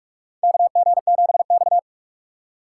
Morse code, heard as a keyed tone, about 44 wpm, 700 Hz. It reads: KG7X
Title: USA Call Signs - 4 Characters - 6x Character Spacing - 45wpm